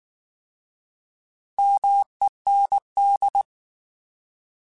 Morse code, heard as a keyed tone, about 19 wpm, 800 Hz. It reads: MEND
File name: Alphabetical / Alphabetical mp3s / Ms @18